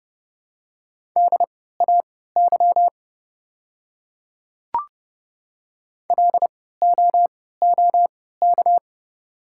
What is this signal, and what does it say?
Morse code, heard as a keyed tone, about 29 wpm, spaced as for 17 wpm, 690 Hz. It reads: DAY LOOK
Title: 3x Character Spacing - Top 100 Words - 30wpm